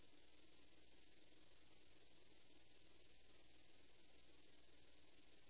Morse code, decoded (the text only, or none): none